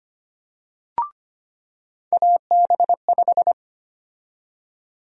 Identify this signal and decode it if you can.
Morse code, heard as a keyed tone, about 25 wpm, 700 Hz. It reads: AB5
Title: USA Call Sign Prefixes - 25wpm